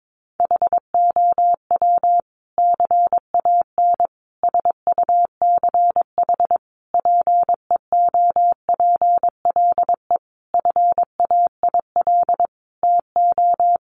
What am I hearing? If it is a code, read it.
Morse code, heard as a keyed tone, about 22 wpm, 700 Hz. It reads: HOW CAN SUCH PEOPLE FAIL TO
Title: Sentences from Top 1100 Words 22wpm